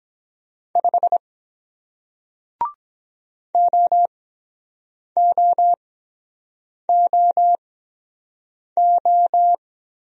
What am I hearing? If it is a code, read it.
Morse code, heard as a keyed tone, about 23 wpm, spaced as for 7 wpm, 700 Hz. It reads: 5 OOOO